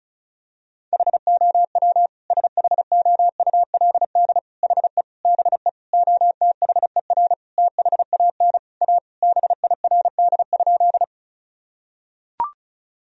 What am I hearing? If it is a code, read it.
Morse code, heard as a keyed tone, about 34 wpm, 700 Hz. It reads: HOW SHOULD HE BE OTHER THAN A BIRD?
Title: Sentences from Top 1000 Words - Review Part 2 - 35wpm